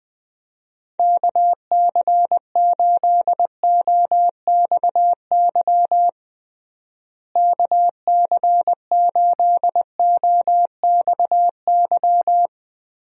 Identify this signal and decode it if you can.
Morse code, heard as a keyed tone, about 21 wpm, 700 Hz. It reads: KC8OXY KC8OXY
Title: USA Call Signs - 6 Characters - Repeated 3x - 20wpm